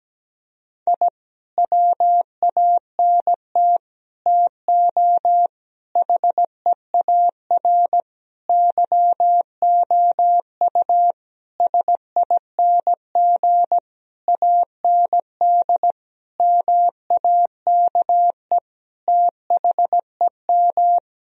Morse code, heard as a keyed tone, about 17 wpm, 700 Hz. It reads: I WANT TO HEAR YOU SING AND MAKE THEM
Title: Sentences from Top 1000 Words - Review Part 4 - 17wpm